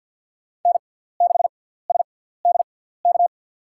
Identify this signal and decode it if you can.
Morse code, heard as a keyed tone, about 51 wpm, spaced as for 17 wpm, 700 Hz. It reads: N6SDK
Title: USA Call Signs - 5 Characters - 6x Character Spacing - 50wpm